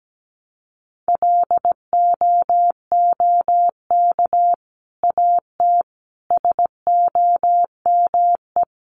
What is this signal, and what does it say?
Morse code, heard as a keyed tone, about 17 wpm, 700 Hz. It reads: LOOK AT SOME
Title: Sets of 5 Words Encore 17wpm